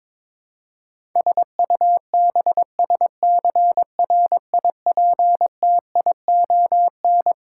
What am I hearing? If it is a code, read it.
Morse code, heard as a keyed tone, about 22 wpm, 700 Hz. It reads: SUBSCRIPTION
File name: Twelve Letter Words - 22wpm